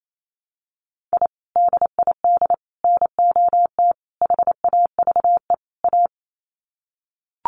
Morse code, heard as a keyed tone, about 28 wpm, 700 Hz. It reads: I DID NOT HAVE A
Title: Sets of 5 Words 28wpm